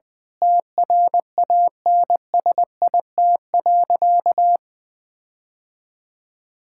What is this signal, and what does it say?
Morse code, heard as a keyed tone, about 20 wpm, 700 Hz. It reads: TRANSIT.
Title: Difficult Sentences with Hints - 20wpm